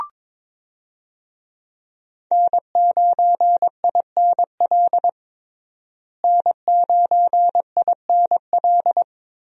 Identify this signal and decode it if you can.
Morse code, heard as a keyed tone, about 22 wpm, 700 Hz. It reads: N9INL N9INL
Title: USA Call Signs - 5 Characters - Repeated 3x - 22wpm